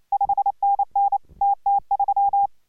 Morse code, heard as a keyed tone, about 28 wpm, 810 Hz. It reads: 5NN TT3